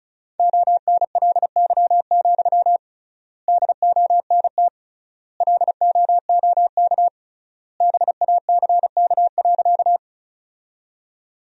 Morse code, heard as a keyed tone, about 36 wpm, 700 Hz. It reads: ONLY, DONT LOOK BACK.
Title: 3x Word-Spacing - Sentences from Top 300 Words - 35wpm